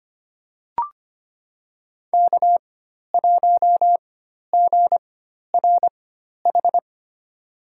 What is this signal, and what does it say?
Morse code, heard as a keyed tone, about 25 wpm, spaced as for 12 wpm, 700 Hz. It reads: K1GRH
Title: USA Call Signs - 5 Characters - 4x Character Spacing - 25wpm